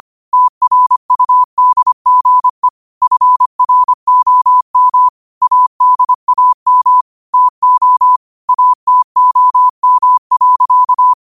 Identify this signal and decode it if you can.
Morse code, heard as a keyed tone, about 25 wpm, 1000 Hz. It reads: TRUDGE FROM ADAM TO ATOM.
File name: qotdfiles / 25wpm